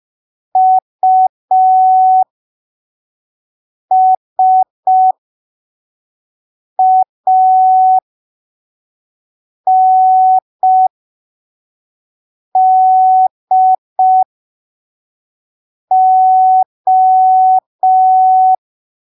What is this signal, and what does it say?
Morse code, heard as a keyed tone, about 5 wpm, spaced as for 3 wpm, 750 Hz. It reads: USANDO
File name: arquivos / texto / texto 04 5ppm